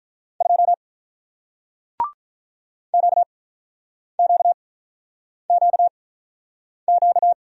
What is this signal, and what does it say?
Morse code, heard as a keyed tone, about 46 wpm, spaced as for 9 wpm, 700 Hz. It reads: 2 QQQQ